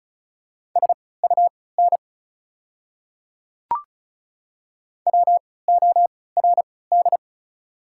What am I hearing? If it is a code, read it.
Morse code, heard as a keyed tone, about 36 wpm, spaced as for 20 wpm, 700 Hz. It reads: SUN WORD
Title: 3x Character Spacing - Top 300 Words - 35wpm